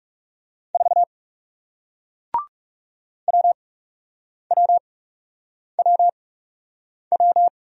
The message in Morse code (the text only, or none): 4 WWWW